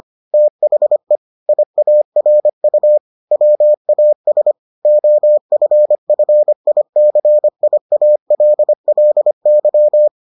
Morse code, heard as a keyed tone, about 25 wpm, 600 Hz. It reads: THE IARU WAS OFFICIALLY